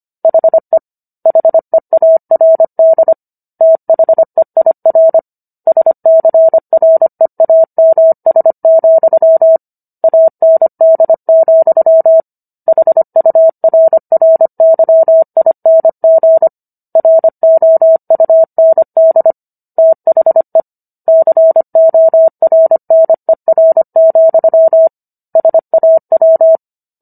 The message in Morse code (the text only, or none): HE HEARD THEIR SCREAMS, AND, HURRYING ROUND THE CORNER, SAW